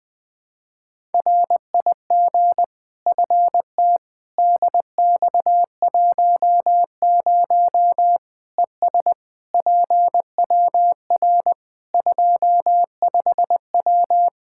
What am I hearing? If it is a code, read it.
Morse code, heard as a keyed tone, about 20 wpm, 700 Hz. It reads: RIG FT DX10 ES PWR 25W
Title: Practice QSO - Exchange 2 - Info - 20wpm